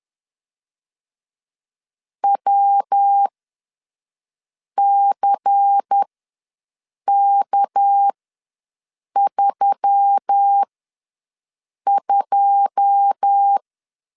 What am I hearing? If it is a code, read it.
Morse code, heard as a keyed tone, about 10 wpm, spaced as for 5 wpm, 800 Hz. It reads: WCK32
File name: LEZIONI / 9 2 ws 10wpm 4130